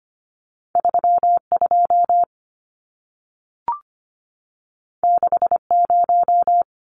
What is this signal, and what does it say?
Morse code, heard as a keyed tone, about 25 wpm, 700 Hz. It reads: 32 60
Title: Numbers - 2-Digit 25wpm